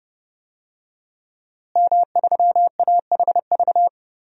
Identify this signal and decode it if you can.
Morse code, heard as a keyed tone, about 30 wpm, 700 Hz. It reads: M3AHV
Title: DXCC-100 Call Signs - 5 Characters - Repeated 2x - 30wpm